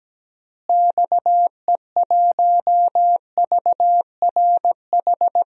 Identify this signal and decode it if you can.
Morse code, heard as a keyed tone, about 17 wpm, 700 Hz. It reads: XE1VRH
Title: DXCC-100 Call Signs - 6 Characters - Repeated 3x - 17wpm